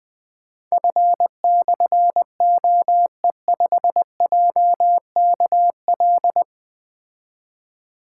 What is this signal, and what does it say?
Morse code, heard as a keyed tone, about 20 wpm, 700 Hz. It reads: F/OE5JKL
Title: Difficult Call Signs 20wpm